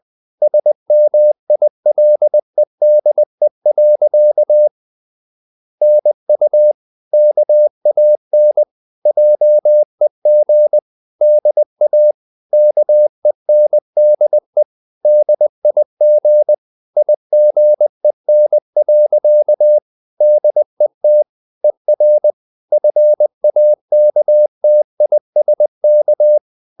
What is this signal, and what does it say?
Morse code, heard as a keyed tone, about 20 wpm, 600 Hz. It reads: SMILEDE. NU KAN JEG DA KENDE DIG IGEN. DET ER FAKTISK
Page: CW med Gnister